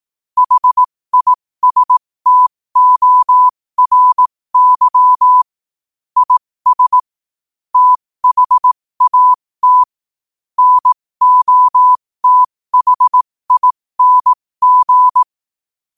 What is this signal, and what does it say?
Morse code, heard as a keyed tone, about 18 wpm, spaced as for 15 wpm, 1000 Hz. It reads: HISTORY IS THAT NOTHING
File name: qotdfiles / 15wpm